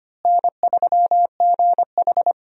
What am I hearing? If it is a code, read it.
Morse code, heard as a keyed tone, about 25 wpm, 700 Hz. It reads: N3GH
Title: USA Call Signs - 4 Characters - Repeated 3x - 25wpm